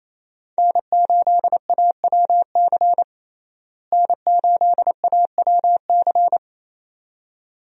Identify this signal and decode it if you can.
Morse code, heard as a keyed tone, about 28 wpm, 700 Hz. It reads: N8AWC N8AWC
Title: USA Call Signs - 5 Characters - Repeated 2x - 28wpm